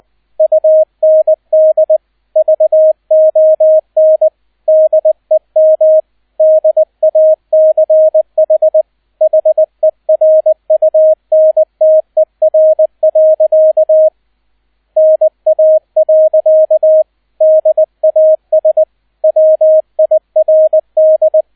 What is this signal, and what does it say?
Morse code, heard as a keyed tone, about 20 wpm, 630 Hz, with radio noise. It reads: UND VON DEM DACH HERUNTER. NA. DAS WIRD
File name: 002 Zweiter Streich 20wpm